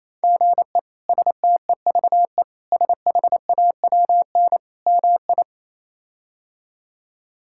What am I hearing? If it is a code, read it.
Morse code, heard as a keyed tone, about 28 wpm, 700 Hz. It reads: GE STEVE SHAWN MI
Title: K1USN SST QSO Practice - 28wpm